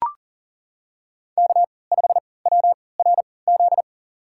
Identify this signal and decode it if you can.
Morse code, heard as a keyed tone, about 37 wpm, spaced as for 22 wpm, 700 Hz. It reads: K5WRZ
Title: USA Call Signs - 5 Characters - 3x Character Spacing - 40wpm